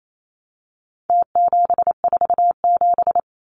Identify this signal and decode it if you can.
Morse code, heard as a keyed tone, about 28 wpm, 700 Hz. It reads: T747
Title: Numbers - Contest Style - 4-Digit 28wpm